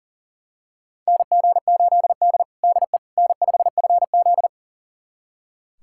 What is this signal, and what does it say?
Morse code, heard as a keyed tone, about 40 wpm, 700 Hz. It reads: NG8D DE N5FZ
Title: USA Callsign DE Callsign - 4 Characters - 40wpm